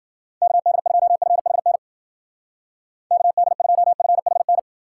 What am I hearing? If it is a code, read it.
Morse code, heard as a keyed tone, about 53 wpm, 700 Hz. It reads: KD2FHN KD2FHN